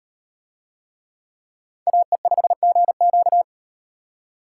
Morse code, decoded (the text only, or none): AE5GQ